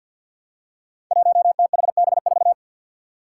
Morse code, heard as a keyed tone, about 53 wpm, 700 Hz. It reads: 1THB4
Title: Five Challenging Letter-Number Combinations 50wpm